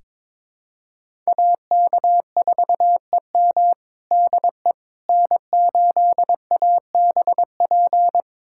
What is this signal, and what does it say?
Morse code, heard as a keyed tone, about 22 wpm, 700 Hz. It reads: AK4EM DE N8ABP
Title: USA Callsign DE Callsign - 5 Characters - 22wpm